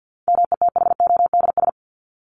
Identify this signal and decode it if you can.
Morse code, heard as a keyed tone, about 50 wpm, 700 Hz. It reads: METHODS
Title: ES:S Suffix - Top 100 Words with Base Words 50wpm